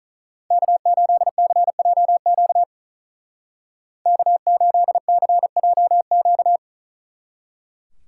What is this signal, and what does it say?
Morse code, heard as a keyed tone, about 43 wpm, 700 Hz. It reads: K8CJQ K8CJQ
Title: Speed Racing - USA Call Signs 35wpm